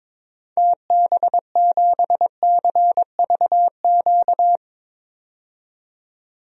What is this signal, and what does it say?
Morse code, heard as a keyed tone, about 22 wpm, 700 Hz. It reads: TB7CVQ
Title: DXCC-100 Call Signs - 6 Characters - 22wpm